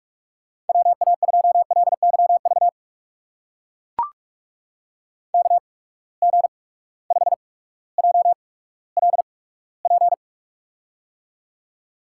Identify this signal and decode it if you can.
Morse code, heard as a keyed tone, about 44 wpm, spaced as for 19 wpm, 700 Hz. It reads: WA2LYV KG5JLP
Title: USA Call Signs - 6 Characters - 8x Character Spacing - 45wpm